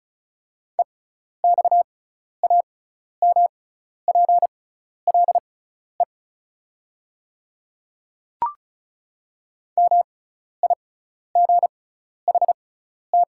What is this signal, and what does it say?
Morse code, heard as a keyed tone, about 34 wpm, spaced as for 12 wpm, 690 Hz. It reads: EXAMPLE MIGHT
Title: 6x Character Spacing - Top 500 Words - 35wpm